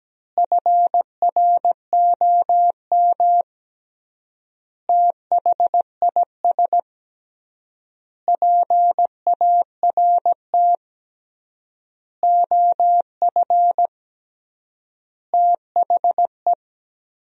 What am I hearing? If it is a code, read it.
Morse code, heard as a keyed tone, about 17 wpm, 700 Hz. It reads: FROM THIS PART OF THE
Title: 3x Word-Spacing - Sentences from Top 300 Words - 17wpm